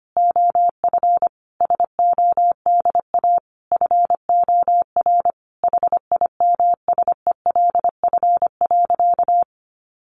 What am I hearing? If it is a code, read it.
Morse code, heard as a keyed tone, about 25 wpm, 700 Hz. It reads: OF SODA FOR HIMSELF.
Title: Additional Easy Phrases 25wpm